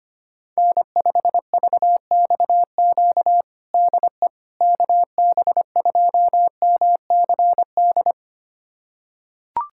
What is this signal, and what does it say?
Morse code, heard as a keyed tone, about 25 wpm, 700 Hz. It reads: N5VXQ DE KB2MCD
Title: USA Callsign DE Callsign 25wpm